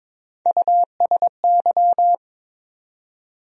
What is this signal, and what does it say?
Morse code, heard as a keyed tone, about 22 wpm, 700 Hz. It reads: USY
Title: USA Call Sign Postfixes - 3 Letters 22wpm